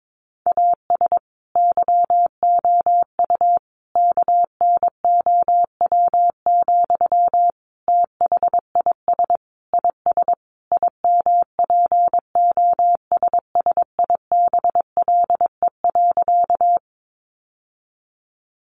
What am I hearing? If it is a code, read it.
Morse code, heard as a keyed tone, about 22 wpm, 700 Hz. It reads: AS YOU KNOW, THIS IS IMPOSSIBLE.
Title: Sentences from Top 1100 Words 22wpm